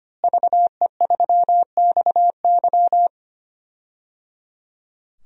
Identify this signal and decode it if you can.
Morse code, heard as a keyed tone, about 25 wpm, 700 Hz. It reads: VE3XY
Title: DXCC-100 Call Signs - 5 Characters - Repeated 3x - 25wpm